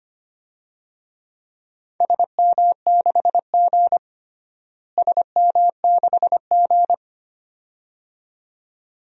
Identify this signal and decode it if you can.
Morse code, heard as a keyed tone, about 25 wpm, 700 Hz. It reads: SM6G SM6G
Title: DXCC-100 Call Signs - 4 Characters - Repeated 2x - 25wpm